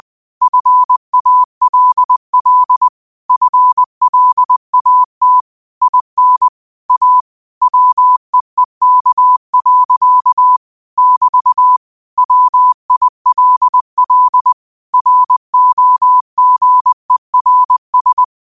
Morse code, heard as a keyed tone, about 20 wpm, 1000 Hz. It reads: FALL FLAT IN A WEEK. <BT> WILL ROGERS
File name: qotdfiles / 20wpm